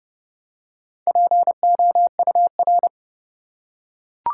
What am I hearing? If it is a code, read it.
Morse code, heard as a keyed tone, about 31 wpm, 700 Hz. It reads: POUR
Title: Four Letter Words - 30wpm